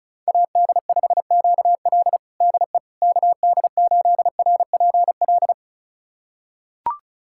Morse code, heard as a keyed tone, about 36 wpm, 700 Hz. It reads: AD5QL DE KD8RPL
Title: USA Callsign DE Callsign 35wpm